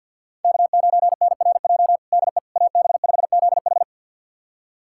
Morse code, heard as a keyed tone, about 50 wpm, 700 Hz. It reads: K9NRJ DE AB5ZH
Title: USA Callsign DE Callsign - 5 Characters - 50wpm